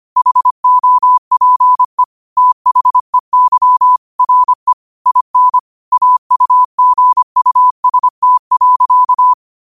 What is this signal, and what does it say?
Morse code, heard as a keyed tone, about 25 wpm, 1000 Hz. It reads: HOPE THEY RE IN AUGUST.